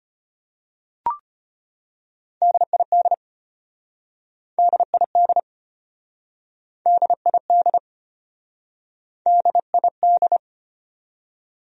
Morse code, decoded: DID DID DID DID

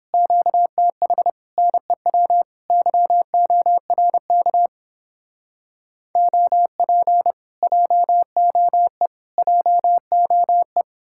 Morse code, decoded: QTH NEW YORK OP JOE JOE

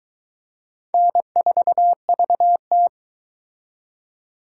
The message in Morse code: N4VT

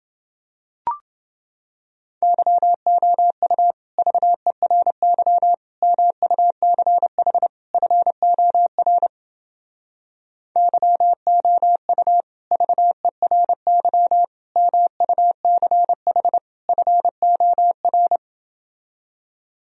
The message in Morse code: YOU VERY MUCH FOR YOU VERY MUCH FOR

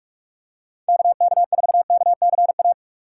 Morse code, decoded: KK4KCA

